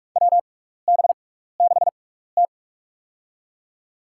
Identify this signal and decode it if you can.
Morse code, heard as a keyed tone, about 47 wpm, spaced as for 16 wpm, 700 Hz. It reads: WB6T
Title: USA Call Signs - 4 Characters - 6x Character Spacing - 45wpm